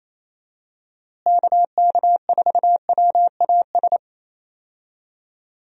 Morse code, heard as a keyed tone, about 28 wpm, 700 Hz. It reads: KK4WAS
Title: USA Call Signs - 6 Characters - Repeated 3x - 28wpm